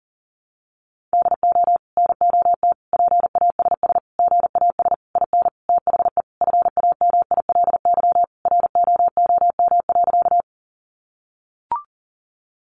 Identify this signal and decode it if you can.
Morse code, heard as a keyed tone, about 40 wpm, 700 Hz. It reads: DO NOT PASS GAS IN THE FAMILY ROOM.
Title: Sentences from Top 1100 Words 40wpm